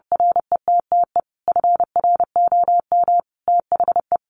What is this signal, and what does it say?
Morse code, heard as a keyed tone, about 30 wpm, 700 Hz. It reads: RETTE FROM THE
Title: Sentences from Top 1000 Words 30wpm